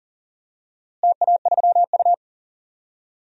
Morse code, heard as a keyed tone, about 40 wpm, 700 Hz. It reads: TA3U
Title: DXCC-100 Call Signs - 4 Characters - Repeated 2x - 40wpm